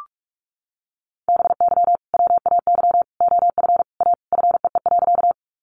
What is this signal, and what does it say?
Morse code, heard as a keyed tone, about 44 wpm, 700 Hz. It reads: BY WAY OF A FEE.
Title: Sentences from Top 1500 Words 45wpm